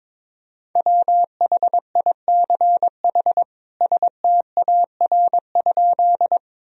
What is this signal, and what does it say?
Morse code, heard as a keyed tone, about 22 wpm, 700 Hz. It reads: WHICH STAR?
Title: Sentences from Top 1000 Words - Review Part 2 - 22wpm